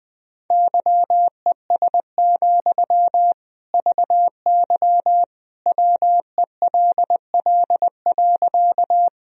Morse code, heard as a keyed tone, about 20 wpm, 700 Hz. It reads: YES, VY WELL.
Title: Sentences from Top 500 Words and Abbreviations 20wpm